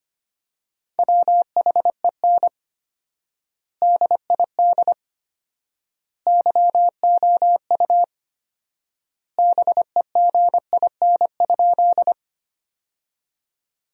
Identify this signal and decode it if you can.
Morse code, heard as a keyed tone, about 25 wpm, 700 Hz. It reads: WHEN DID YOU BEGIN?
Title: 4x Word-Spacing - Sentences from Top 300 Words - 25wpm